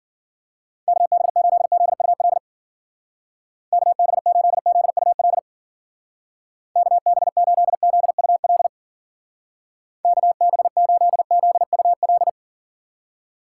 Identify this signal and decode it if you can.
Morse code, heard as a keyed tone, about 54 wpm, 700 Hz. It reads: KB8ZUL KB8ZUL KB8ZUL KB8ZUL